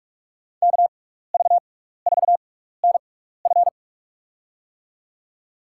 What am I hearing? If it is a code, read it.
Morse code, heard as a keyed tone, about 43 wpm, spaced as for 15 wpm, 700 Hz. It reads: KV4NF
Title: USA Call Signs - 5 Characters - 6x Character Spacing - 45wpm